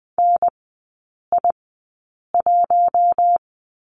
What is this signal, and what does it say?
Morse code, heard as a keyed tone, about 20 wpm, spaced as for 8 wpm, 700 Hz. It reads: NI1
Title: 047 Lesson 9e - Number 4 - Three Characters 20wpm